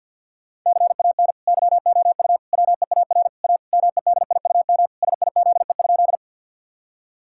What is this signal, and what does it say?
Morse code, heard as a keyed tone, about 50 wpm, 700 Hz. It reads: CAN YOU WEAR A MEDIUM SIZE?
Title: Sentences from Top 1000 Words - Review Part 6 - 50wpm